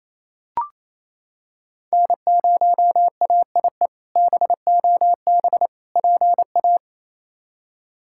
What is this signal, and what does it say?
Morse code, heard as a keyed tone, about 28 wpm, 700 Hz. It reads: N0AIE BOB PA
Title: K1USN SST QSO Practice - 28wpm